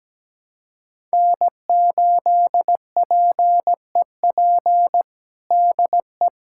Morse code, heard as a keyed tone, about 17 wpm, 700 Hz. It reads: N8PEP DE
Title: USA Callsign DE Callsign - 5 Characters - 17wpm